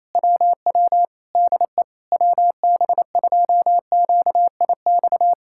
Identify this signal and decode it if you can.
Morse code, heard as a keyed tone, about 27 wpm, 700 Hz. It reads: WW DE WB2QIX
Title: Speed-Racing Common QSO Phrases - 28wpm